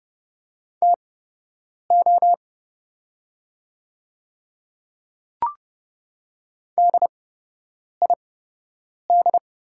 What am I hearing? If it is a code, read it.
Morse code, heard as a keyed tone, about 29 wpm, spaced as for 8 wpm, 700 Hz. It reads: TO DID
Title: 8x Character Spacing - Top 100 Words - 30wpm